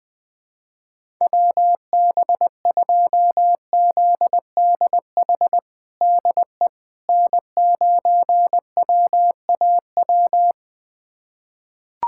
Morse code, heard as a keyed tone, about 21 wpm, 700 Hz. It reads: WB2ZDH DE N9WAW
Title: USA Callsign DE Callsign 20wpm